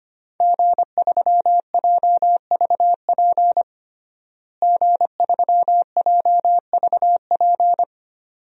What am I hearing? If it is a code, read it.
Morse code, heard as a keyed tone, about 25 wpm, 700 Hz. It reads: G3JVP G3JVP
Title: DXCC-100 Call Signs - 5 Characters - Repeated 2x - 25wpm